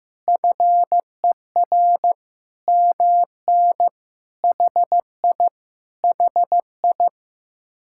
Morse code, heard as a keyed tone, about 15 wpm, 700 Hz. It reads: FER MN HI HI
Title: Speed-Racing Common QSO Phrases Encore - 15wpm